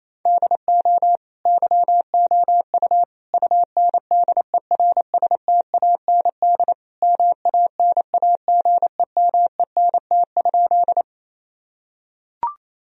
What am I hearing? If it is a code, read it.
Morse code, heard as a keyed tone, about 28 wpm, 700 Hz. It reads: DO YOU UNDERSTAND MANAGEMENT?
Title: Sentences from Top 1000 Words - Review Part 4 - 28wpm